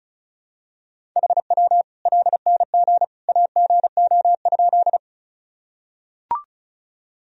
Morse code, heard as a keyed tone, about 36 wpm, 700 Hz. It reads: HW LNG AGO?
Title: Sentences from Top 500 Words and Abbreviations 35wpm